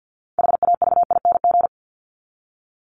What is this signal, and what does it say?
Morse code, heard as a keyed tone, about 50 wpm, 700 Hz. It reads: HAVING
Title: ING Suffix - Top 100 Words with Base Words 50wpm